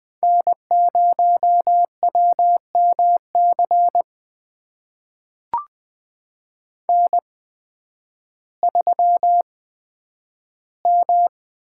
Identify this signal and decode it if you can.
Morse code, heard as a keyed tone, about 20 wpm, spaced as for 11 wpm, 700 Hz. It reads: N0WMC N3M
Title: USA Call Signs - 5 Characters - 8x Character Spacing - 20wpm